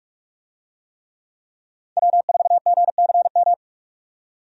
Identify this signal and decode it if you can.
Morse code, heard as a keyed tone, about 46 wpm, 700 Hz. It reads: W4GCM
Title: USA Call Signs - 5 Characters - 45wpm